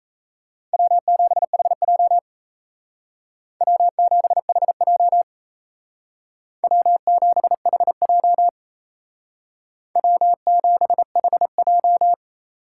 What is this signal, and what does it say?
Morse code, heard as a keyed tone, about 35 wpm, 700 Hz. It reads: W7HJ W7HJ W7HJ W7HJ
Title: Speed Racing - USA Call Signs 28wpm